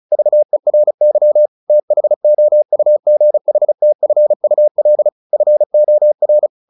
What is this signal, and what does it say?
Morse code, heard as a keyed tone, about 34 wpm, 600 Hz. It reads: VERY THOUGHTFUL FOR
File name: the old man and the sea / the old man and the sea 1 35wpm